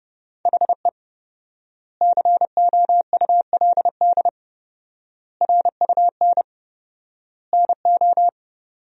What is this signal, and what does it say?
Morse code, heard as a keyed tone, about 30 wpm, 700 Hz. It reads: HE COULD RUN NO
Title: 4x Word-Spacing - Sentences from Top 300 Words - 30wpm